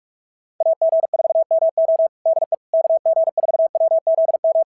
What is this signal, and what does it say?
Morse code, heard as a keyed tone, about 44 wpm, 650 Hz. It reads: AG4MO DE KG4WZM
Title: SOTA QSO Practice 45wpm